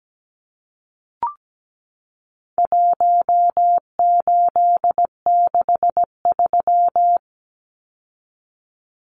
Morse code, heard as a keyed tone, about 17 wpm, 700 Hz. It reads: 1863
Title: Numbers - Contest Style - 4-Digit 17wpm